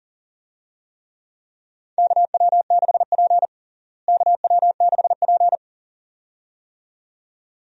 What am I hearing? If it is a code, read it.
Morse code, heard as a keyed tone, about 40 wpm, 700 Hz. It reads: KW6P KW6P
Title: USA Call Signs - 4 Characters - Repeated 2x - 40wpm